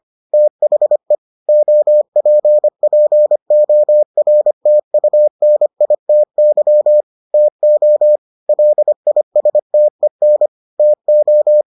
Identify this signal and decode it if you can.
Morse code, heard as a keyed tone, about 25 wpm, 600 Hz. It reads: THE OPPORTUNITY TO LISTEN TO